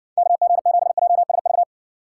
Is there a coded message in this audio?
Yes: Morse code from 0.2 to 1.6 s, heard as a keyed tone, about 60 wpm, 700 Hz. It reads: KG7JSV